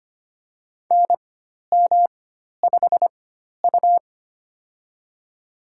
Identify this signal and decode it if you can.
Morse code, heard as a keyed tone, about 25 wpm, spaced as for 12 wpm, 700 Hz. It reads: NM5U
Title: USA Call Signs - 4 Characters - 4x Character Spacing - 25wpm